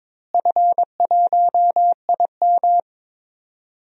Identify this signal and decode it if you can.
Morse code, heard as a keyed tone, about 22 wpm, 700 Hz. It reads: F1IM
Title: DXCC-100 Call Signs - 4 Characters - Repeated 3x - 22wpm